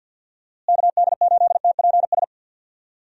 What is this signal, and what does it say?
Morse code, heard as a keyed tone, about 52 wpm, 700 Hz. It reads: KD8TPS
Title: USA Call Signs - 6 Characters - Repeated 2x - 50wpm